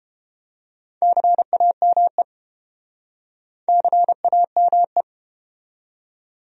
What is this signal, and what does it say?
Morse code, heard as a keyed tone, about 31 wpm, 700 Hz. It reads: CAME CAME